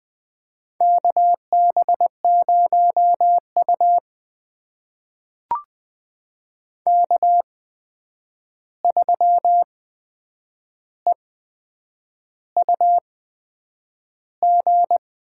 Morse code, heard as a keyed tone, about 20 wpm, spaced as for 8 wpm, 700 Hz. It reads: KB0U K3EUG
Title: USA Call Signs - 8x Character Spacing - 20wpm